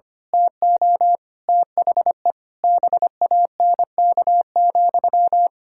Morse code, heard as a keyed tone, about 25 wpm, 700 Hz. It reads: TO THE BANK,
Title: Pangram Challenge v2 - 25wpm